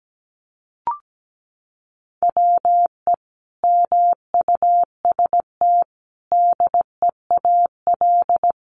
Morse code, heard as a keyed tone, about 17 wpm, 700 Hz. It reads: WE MUST DEAL
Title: Sentences from Top 1400 Words 17wpm